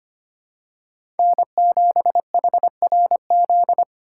Morse code, heard as a keyed tone, about 25 wpm, 700 Hz. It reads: N7HRZ
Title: USA Call Signs - 5 Characters - 4x Character Spacing - 25wpm